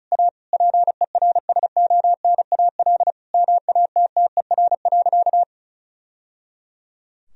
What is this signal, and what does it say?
Morse code, heard as a keyed tone, about 36 wpm, 700 Hz. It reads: A PERSONAL MATTER.
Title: Sentences from Top 1000 Words - Review Part 4 - 35wpm